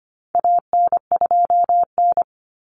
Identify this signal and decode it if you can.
Morse code, heard as a keyed tone, about 25 wpm, 700 Hz. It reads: AN2N
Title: Common Cut Numbers - 4-Digit 25wpm